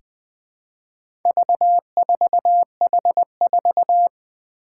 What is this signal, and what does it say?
Morse code, heard as a keyed tone, about 20 wpm, 700 Hz. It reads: V4H4
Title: Four Challenging Letter-Number Combinations 20wpm